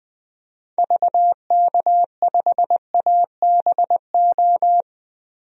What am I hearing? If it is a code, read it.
Morse code, heard as a keyed tone, about 21 wpm, 700 Hz. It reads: VK5ABO
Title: DXCC-100 Call Signs - 6 Characters - 20wpm